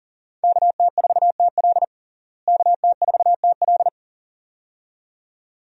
Morse code, heard as a keyed tone, about 40 wpm, 700 Hz. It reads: KT4TL KT4TL
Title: USA Call Signs - 5 Characters - Repeated 2x - 40wpm